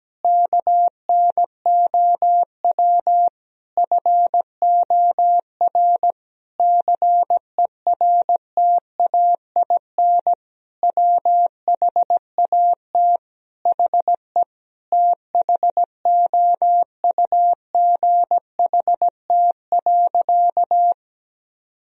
Morse code, heard as a keyed tone, about 17 wpm, 700 Hz. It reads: KNOW FOR CERTAIN WHAT HE THOUGHT.
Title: Sentences from Top 1000 Words - Review Part 2 - 17wpm